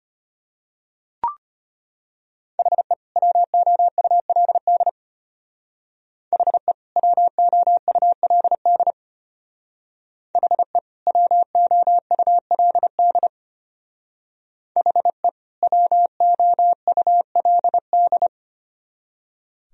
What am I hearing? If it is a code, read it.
Morse code, heard as a keyed tone, about 31 wpm, 700 Hz. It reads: HE WOULD HE WOULD HE WOULD HE WOULD
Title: Speed-Racing - Sets of 2 Words 25wpm